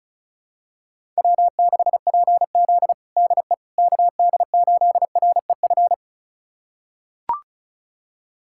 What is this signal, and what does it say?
Morse code, heard as a keyed tone, about 36 wpm, 700 Hz. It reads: W6PZ DE KD8REF
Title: USA Callsign DE Callsign 35wpm